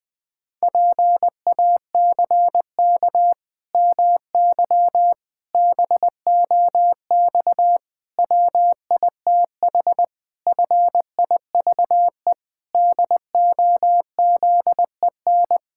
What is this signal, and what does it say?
Morse code, heard as a keyed tone, about 19 wpm, 700 Hz. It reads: PACK MY BOX WITH FIVE DOZEN
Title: Pangram Challenge v2 - 20wpm